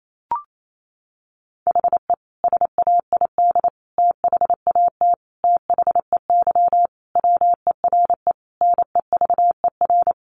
Morse code, heard as a keyed tone, about 28 wpm, 700 Hz. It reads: HE SAID THAT THEY WERE NEVER
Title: Sentences from Top 800 Words 28wpm